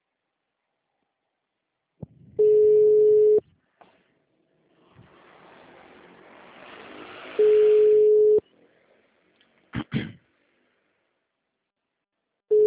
Ringback tone (Europe, 425 Hz).